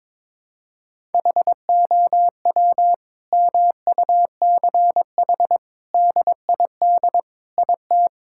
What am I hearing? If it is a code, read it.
Morse code, heard as a keyed tone, about 21 wpm, 700 Hz. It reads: HOW MUCH DID IT
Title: Sentences from Top 1000 Words - Review Part 5 - 22wpm